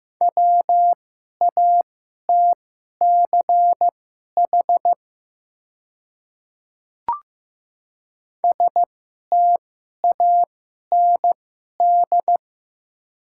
Morse code, heard as a keyed tone, about 15 wpm, spaced as for 11 wpm, 700 Hz. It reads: WATCH STAND
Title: 2x Character Spacing - Top 300 Words - 15wpm